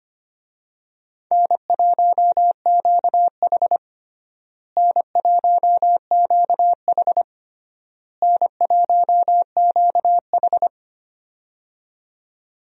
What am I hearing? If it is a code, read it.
Morse code, heard as a keyed tone, about 25 wpm, 700 Hz. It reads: N1QH N1QH N1QH
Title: USA Call Signs - 4 Characters - Repeated 3x - 25wpm